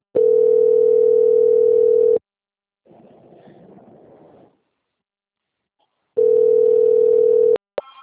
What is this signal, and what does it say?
Ringback tone (North America), 440 + 480 Hz.
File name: out- - - - - .28714